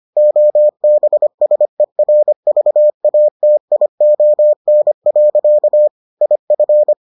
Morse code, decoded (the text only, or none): OBSERVATION. IF